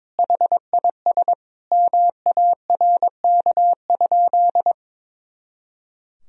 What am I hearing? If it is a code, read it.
Morse code, heard as a keyed tone, about 22 wpm, 700 Hz. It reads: HIS MARK?
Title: Sentences from Top 300 Words and Abbreviations 22wpm